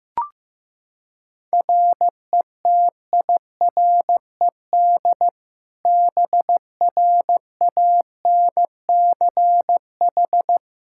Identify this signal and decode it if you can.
Morse code, heard as a keyed tone, about 15 wpm, 700 Hz. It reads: RETIRED BRANCH
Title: Practice QSO - Exchange 3 - More Info - 15wpm